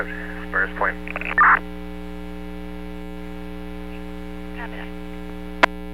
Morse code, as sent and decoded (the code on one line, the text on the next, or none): none